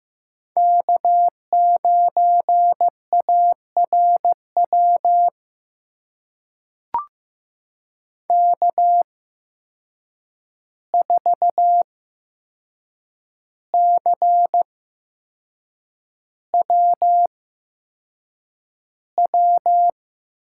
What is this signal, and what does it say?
Morse code, heard as a keyed tone, about 15 wpm, spaced as for 6 wpm, 700 Hz. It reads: K9ARW K4CWW
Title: USA Call Signs - 5 Characters - 8x Character Spacing - 15wpm